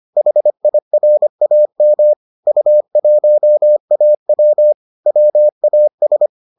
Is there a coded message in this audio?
Yes: Morse code from 0.2 to 6.3 s, heard as a keyed tone, about 25 wpm, 600 Hz. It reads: HIRAM U1AW WAS